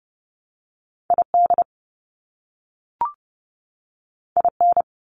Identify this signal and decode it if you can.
Morse code, heard as a keyed tone, about 30 wpm, 700 Hz. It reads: ID IN
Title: US States 30wpm